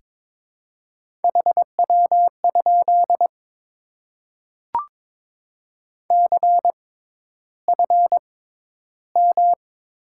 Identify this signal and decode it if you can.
Morse code, heard as a keyed tone, about 22 wpm, spaced as for 11 wpm, 700 Hz. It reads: HW? CFM
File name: Top 90 QSO Elements - 6x Character Spacing - 22wpm